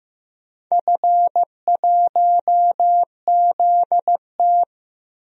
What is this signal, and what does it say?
Morse code, heard as a keyed tone, about 15 wpm, 700 Hz. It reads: F1ZT
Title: DXCC-100 Call Signs - 4 Characters - Repeated 2x - 15wpm